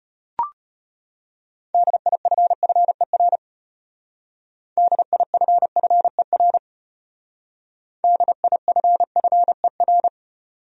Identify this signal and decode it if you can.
Morse code, heard as a keyed tone, about 33 wpm, 700 Hz. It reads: DIFFER DIFFER DIFFER